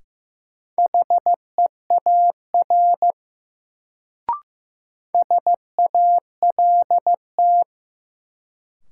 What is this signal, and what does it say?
Morse code, heard as a keyed tone, about 16 wpm, 700 Hz. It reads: HEAR SALT
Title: Sets of 5 Words - New Word Drill - 15wpm